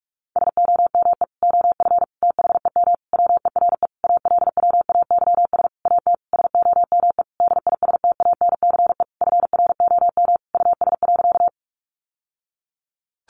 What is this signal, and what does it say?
Morse code, heard as a keyed tone, about 46 wpm, 700 Hz. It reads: SOME OF THEM WERE ALWAYS AT SOME DISTANCE FROM US.
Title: Sentences from Top 1200 Words 45wpm